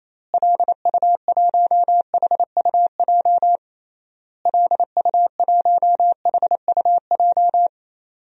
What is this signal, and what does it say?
Morse code, heard as a keyed tone, about 28 wpm, 700 Hz. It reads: LU1HUJ LU1HUJ
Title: DXCC-100 Call Signs - 6 Characters - Repeated 3x - 28wpm